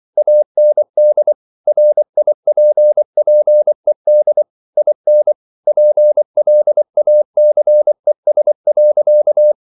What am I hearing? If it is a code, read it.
Morse code, heard as a keyed tone, about 24 wpm, 600 Hz. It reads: AND RIPPED IN PLACES.